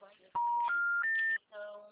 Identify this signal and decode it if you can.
Special information tone, with speech.
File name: out- -2006